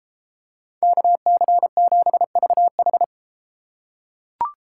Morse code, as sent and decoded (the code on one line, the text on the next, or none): -.- -.-. --... ...- ....
KC7VH